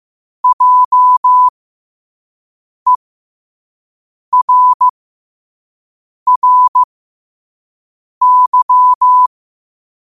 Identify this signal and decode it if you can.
Morse code, heard as a keyed tone, about 15 wpm, spaced as for 5 wpm, 1000 Hz. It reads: JERRY
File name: qotdfiles / 5wpm